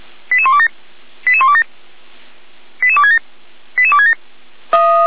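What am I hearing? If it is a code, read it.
Five-tone selective call (ZVEI), heard as quick runs of five tones, tones of about 70 ms. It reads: 88127 88127 88267 88267
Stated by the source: ZVEI five-tone call received by SDR, from audio.f85360.feuerwehr-orlamuende.org, a fire brigade (Feuerwehr Orlamünde) server.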